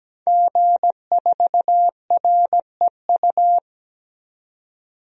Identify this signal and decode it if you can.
Morse code, heard as a keyed tone, about 17 wpm, 700 Hz. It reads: G4REU
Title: DXCC-100 Call Signs - 5 Characters - 17wpm